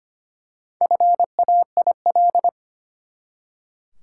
Morse code, heard as a keyed tone, about 25 wpm, 700 Hz. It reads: FAIL
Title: Four Letter Words - 25wpm